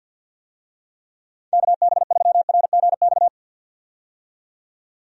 Morse code, heard as a keyed tone, about 47 wpm, 700 Hz. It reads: KB3RGX
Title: USA Call Signs - 6 Characters - 50wpm